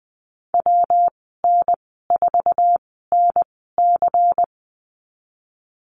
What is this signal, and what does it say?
Morse code, heard as a keyed tone, about 20 wpm, spaced as for 15 wpm, 700 Hz. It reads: WN4NC
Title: 122 Lesson 20g - Number 3 - Call Signs 20wpm